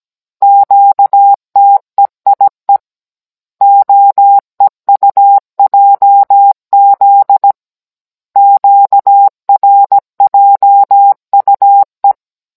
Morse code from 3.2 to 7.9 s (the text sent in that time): OEUJZ